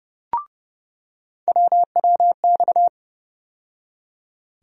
Morse code, heard as a keyed tone, about 30 wpm, 700 Hz. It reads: WWX
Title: USA Call Sign Postfixes - 3 Letters 30wpm